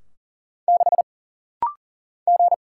Morse code, heard as a keyed tone, about 40 wpm, 700 Hz. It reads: 6 G
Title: Single Letter-Number - Rapid-Fire 40wpm